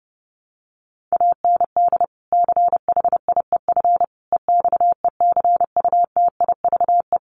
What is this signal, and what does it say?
Morse code, heard as a keyed tone, about 28 wpm, 700 Hz. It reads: AND CHIEF EXECUTIVE
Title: Sets of 5 Words Encore2 30wpm